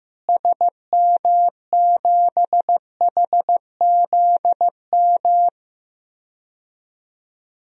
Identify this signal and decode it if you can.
Morse code, heard as a keyed tone, about 15 wpm, 700 Hz. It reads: SM7HZM
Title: DXCC-100 Call Signs - 6 Characters - Repeated 3x - 15wpm